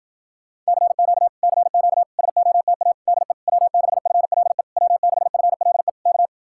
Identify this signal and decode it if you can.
Morse code, heard as a keyed tone, about 52 wpm, 700 Hz. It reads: CQ CQ SOTA DE W6FLE W6FLE K